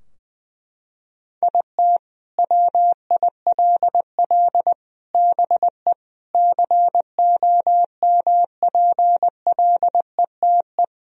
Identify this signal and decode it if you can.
Morse code, heard as a keyed tone, about 20 wpm, 700 Hz. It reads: IT WILL BE COMPLETE
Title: Sentences from Top 1000 Words - Review Part 2 - 20wpm